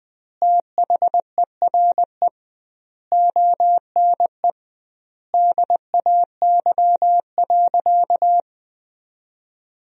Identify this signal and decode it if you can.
Morse code, heard as a keyed tone, about 20 wpm, 700 Hz. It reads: THERE ONE DAY.
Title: 2x Word-Spacing - Sentences from Top 300 Words - 20wpm